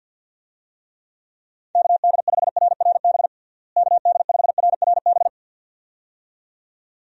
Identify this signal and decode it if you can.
Morse code, heard as a keyed tone, about 51 wpm, 700 Hz. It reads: KD5RRB KD5RRB
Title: USA Call Signs - 6 Characters - Repeated 2x - 50wpm